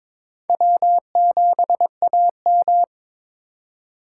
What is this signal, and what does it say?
Morse code, heard as a keyed tone, about 22 wpm, 700 Hz. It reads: W7AM
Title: USA Call Signs - 4 Characters - 3x Character Spacing - 22wpm